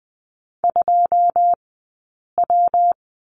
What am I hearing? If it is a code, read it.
Morse code, heard as a keyed tone, about 20 wpm, spaced as for 8 wpm, 700 Hz. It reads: 2W
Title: 165 Lesson 27d - Number 7 - Two Characters 20wpm